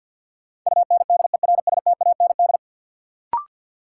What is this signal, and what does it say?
Morse code, heard as a keyed tone, about 49 wpm, 700 Hz. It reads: UNDERSTAND
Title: Sets of 4 Words - New Word Drill - 50wpm